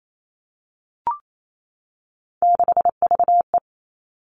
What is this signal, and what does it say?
Morse code, heard as a keyed tone, about 28 wpm, 700 Hz. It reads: 6VE
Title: Three Challenging Letter-Number Combinations 28wpm